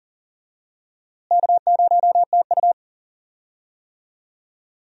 Morse code, heard as a keyed tone, about 40 wpm, 700 Hz. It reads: K0TU
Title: USA Call Signs - 4 Characters - Repeated 2x - 40wpm